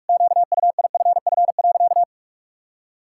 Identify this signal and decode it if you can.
Morse code, heard as a keyed tone, about 46 wpm, 700 Hz. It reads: QUIFF.
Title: Pangram Challenge v2 - 45wpm